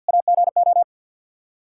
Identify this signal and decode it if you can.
Morse code, heard as a keyed tone, about 52 wpm, 700 Hz. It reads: AGO
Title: K1USN SST QSO Practice 50wpm